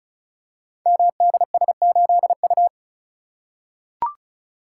Morse code, heard as a keyed tone, about 36 wpm, 690 Hz. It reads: MDS8U
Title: Code Groups - 5 Letter:Numbers - 35wpm